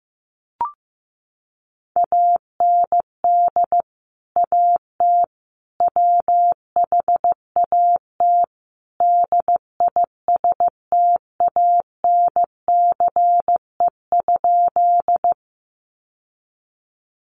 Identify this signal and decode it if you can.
Morse code, heard as a keyed tone, about 15 wpm, 700 Hz. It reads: AND AT WHAT DISTANCE?
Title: Sentences from Top 1200 Words 15wpm